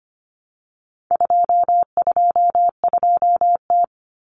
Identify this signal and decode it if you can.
Morse code, heard as a keyed tone, about 25 wpm, 700 Hz. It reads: 222T
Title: Common Cut Numbers - 4-Digit 25wpm